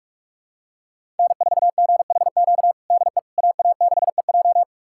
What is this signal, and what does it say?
Morse code, heard as a keyed tone, about 47 wpm, 700 Hz. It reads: N4GHQ DE AA6EJ
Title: USA Callsign DE Callsign - 5 Characters - 45wpm